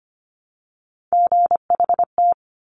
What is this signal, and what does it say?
Morse code, heard as a keyed tone, about 25 wpm, 700 Hz. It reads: GHT
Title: Top Three Letter Combinations in Words 25wpm